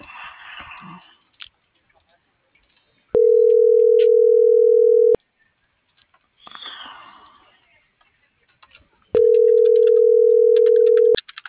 Ringback tone (North America), 440 + 480 Hz.
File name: CENTRAL MULTIMODAL / out angel.velazquez - .42060